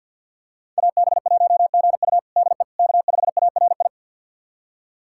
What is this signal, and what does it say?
Morse code, heard as a keyed tone, about 48 wpm, 700 Hz. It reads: AB1GU DE K5ARI
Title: USA Callsign DE Callsign - 5 Characters - 50wpm